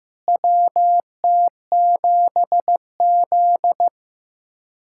Morse code, heard as a keyed tone, about 15 wpm, 700 Hz. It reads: WT7Z
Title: USA Callsign DE Callsign - 4 Characters - 15wpm